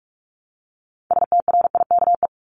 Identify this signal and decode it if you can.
Morse code, heard as a keyed tone, about 44 wpm, 700 Hz. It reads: STRIKE
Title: Top 1501-1750 Words 45wpm